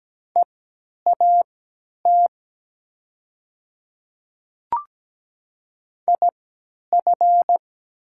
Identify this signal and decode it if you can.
Morse code, heard as a keyed tone, about 17 wpm, spaced as for 10 wpm, 690 Hz. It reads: EAT IF
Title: 3x Character Spacing - Top 300 Words - 17wpm